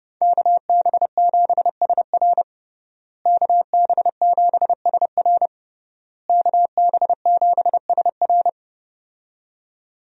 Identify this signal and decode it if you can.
Morse code, heard as a keyed tone, about 32 wpm, 700 Hz. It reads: KB7SR KB7SR KB7SR
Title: USA Call Signs - 5 Characters - Repeated 3x - 30wpm